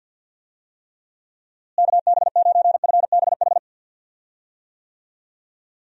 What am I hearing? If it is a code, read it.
Morse code, heard as a keyed tone, about 52 wpm, 700 Hz. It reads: KB9FBH
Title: USA Call Signs - 6 Characters - Repeated 2x - 50wpm